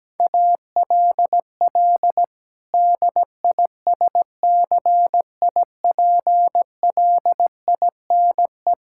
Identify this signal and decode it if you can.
Morse code, heard as a keyed tone, about 17 wpm, 700 Hz. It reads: ALL DISCIPLINE
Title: Sentences from Top 1000 Words - Review Part 6 - 17wpm